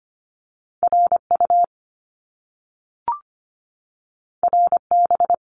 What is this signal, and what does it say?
Morse code, heard as a keyed tone, about 25 wpm, 700 Hz. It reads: RU RB
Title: Top Two Letter Combinations in Words 25wpm